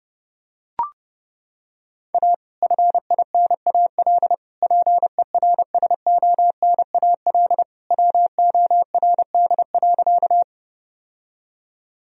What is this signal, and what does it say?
Morse code, heard as a keyed tone, about 32 wpm, 700 Hz. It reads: A FINAL PERSONAL WORD.
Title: Sentences from Top 1000 Words - Review Part 4 - 30wpm